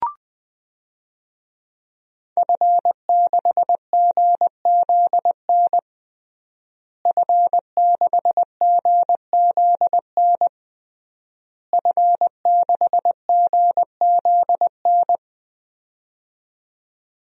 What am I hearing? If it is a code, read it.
Morse code, heard as a keyed tone, about 21 wpm, 700 Hz. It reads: F6GZN F6GZN F6GZN
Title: DXCC-100 Call Signs - 5 Characters - Repeated 3x - 20wpm